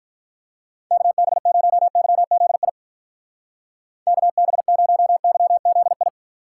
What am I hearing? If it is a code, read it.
Morse code, heard as a keyed tone, about 43 wpm, 700 Hz. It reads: KB0YZI KB0YZI
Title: Speed Racing - USA 2x3 Call Signs 35wpm